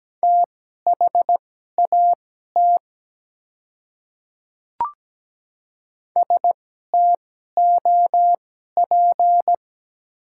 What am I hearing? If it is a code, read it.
Morse code, heard as a keyed tone, about 17 wpm, spaced as for 12 wpm, 700 Hz. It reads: THAT STOP
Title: 2x Character Spacing - Top 300 Words - 17wpm